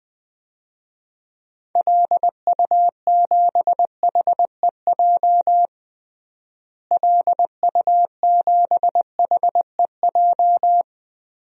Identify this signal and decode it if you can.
Morse code, heard as a keyed tone, about 20 wpm, 700 Hz. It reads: LU7HEJ LU7HEJ
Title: DXCC-100 Call Signs - 6 Characters - Repeated 3x - 20wpm